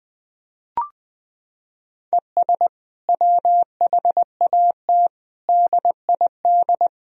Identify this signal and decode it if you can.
Morse code, heard as a keyed tone, about 20 wpm, 700 Hz. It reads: ES WHAT DID
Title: Sentences from Top 100 Words and Abbreviations 20wpm